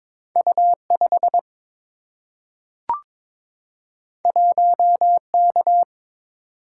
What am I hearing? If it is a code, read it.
Morse code, heard as a keyed tone, about 22 wpm, 690 Hz. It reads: U5 1K
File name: practice / 22wpm / Two Letter-Number Combinations - 22wpm